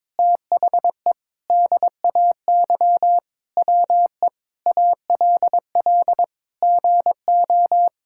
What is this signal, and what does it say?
Morse code, heard as a keyed tone, about 21 wpm, 700 Hz. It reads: THE DAY WE ALL GO
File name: practice / 22wpm / Sentences with 4-Word Phrases - 22wpm